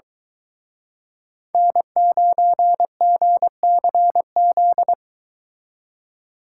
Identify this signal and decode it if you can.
Morse code, heard as a keyed tone, about 23 wpm, 700 Hz. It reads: N9GCZ